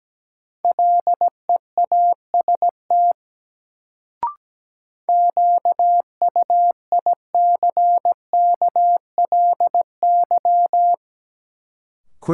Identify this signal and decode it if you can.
Morse code, heard as a keyed tone, about 17 wpm, 700 Hz. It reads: LEAST QUICKLY
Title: Sentences from Top 700 Words - New Word Drill - 17wpm